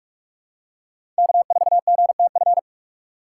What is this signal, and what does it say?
Morse code, heard as a keyed tone, about 46 wpm, 700 Hz. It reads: K4GTF
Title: USA Call Signs - 5 Characters - 2x Character Spacing - 45wpm